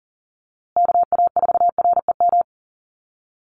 Morse code, heard as a keyed tone, about 40 wpm, 700 Hz. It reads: KA4REM
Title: USA Call Signs Repeated 2x 40wpm